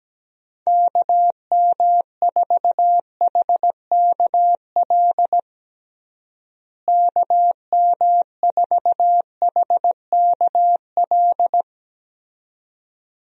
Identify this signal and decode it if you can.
Morse code, heard as a keyed tone, about 17 wpm, 700 Hz. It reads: KM4HKL KM4HKL
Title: USA Call Signs - 6 Characters - Repeated 2x - 17wpm